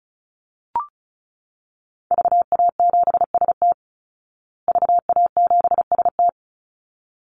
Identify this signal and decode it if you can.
Morse code, heard as a keyed tone, about 34 wpm, 700 Hz. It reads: VA7ST VA7ST
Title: Call Signs Repeated 3x - 35wpm